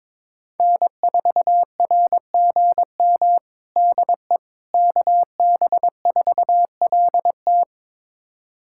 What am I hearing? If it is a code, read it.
Morse code, heard as a keyed tone, about 22 wpm, 700 Hz. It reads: N4RGM DE KB4LT
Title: USA Callsign DE Callsign - 5 Characters - 22wpm